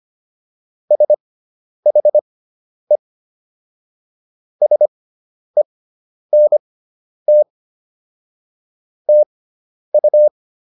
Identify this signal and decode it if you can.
Morse code, heard as a keyed tone, about 25 wpm, spaced as for 10 wpm, 600 Hz. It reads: SHE SENT TU